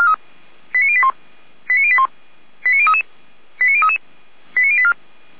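Five-tone selective call (ZVEI), heard as quick runs of five tones, tones of about 70 ms. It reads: 89081 89081 89033 89033 89084
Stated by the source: Five-tone ZVEI selective call received on an SDR.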